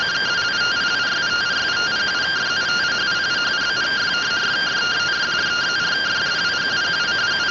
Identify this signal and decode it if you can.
Two-tone teleprinter signal, about 170 Hz shift.